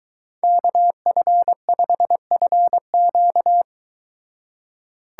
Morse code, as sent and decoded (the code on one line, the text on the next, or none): -.- ..-. ..... ..-. --.-
KF5FQ